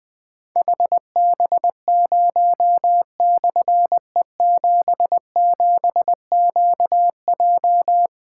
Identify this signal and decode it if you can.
Morse code, heard as a keyed tone, about 21 wpm, 700 Hz. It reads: HB0/E77QJ
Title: Difficult Call Signs 20wpm